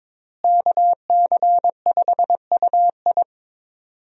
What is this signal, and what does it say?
Morse code, heard as a keyed tone, about 22 wpm, 700 Hz. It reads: KC5UI
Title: USA Call Signs - 5 Characters - Repeated 3x - 22wpm